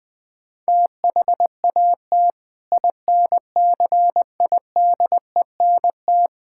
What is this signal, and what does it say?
Morse code, heard as a keyed tone, about 20 wpm, 700 Hz. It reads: THAT INCIDENT
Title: Sentences with 8-Word Phrases - 20wpm